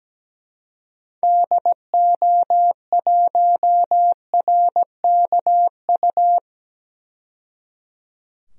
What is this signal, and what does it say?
Morse code, heard as a keyed tone, about 17 wpm, 700 Hz. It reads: DO1RKU
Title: DXCC-100 Call Signs - 6 Characters - 17wpm